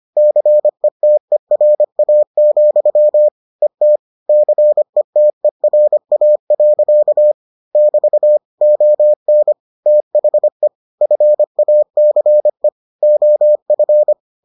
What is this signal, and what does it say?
Morse code, heard as a keyed tone, about 25 wpm, 600 Hz. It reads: CETERA, ET CETERA. <BT> ON THE FACE OF